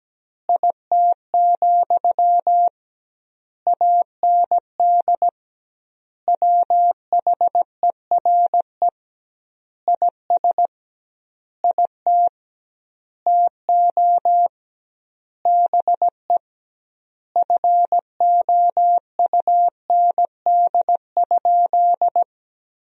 Morse code, heard as a keyed tone, about 17 wpm, 700 Hz. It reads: IT, AND WHERE IS IT TO BE FOUND?
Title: 2x Word-Spacing - Sentences from Top 300 Words - 17wpm